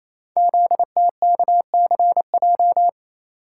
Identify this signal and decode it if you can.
Morse code, heard as a keyed tone, about 27 wpm, 700 Hz. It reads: ZTKCJ
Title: Code Groups - 5 Letters - 28wpm